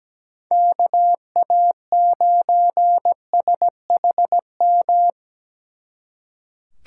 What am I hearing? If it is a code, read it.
Morse code, heard as a keyed tone, about 20 wpm, 700 Hz. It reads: KA9SHM
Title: Speed Racing - USA 2x3 Call Signs 17wpm